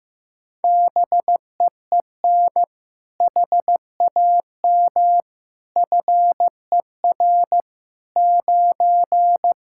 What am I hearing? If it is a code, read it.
Morse code, heard as a keyed tone, about 15 wpm, 700 Hz. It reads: BEEN HAM FER 9
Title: Practice QSO - Exchange 3 - More Info - 15wpm